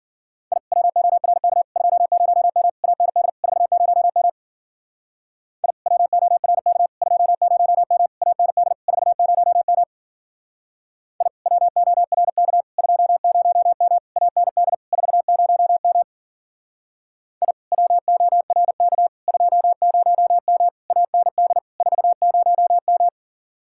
Morse code, heard as a keyed tone, about 52 wpm, 700 Hz. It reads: I WORK 20M AND 40M I WORK 20M AND 40M I WORK 20M AND 40M I WORK 20M AND 40M